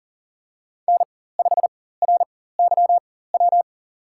Morse code, heard as a keyed tone, about 39 wpm, spaced as for 19 wpm, 700 Hz. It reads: N5RYW
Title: USA Call Signs - 5 Characters - 4x Character Spacing - 40wpm